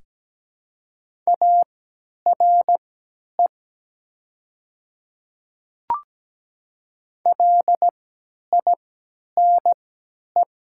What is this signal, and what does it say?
Morse code, heard as a keyed tone, about 17 wpm, spaced as for 10 wpm, 700 Hz. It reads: ARE LINE
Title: 3x Character Spacing - Top 200 Words - 17wpm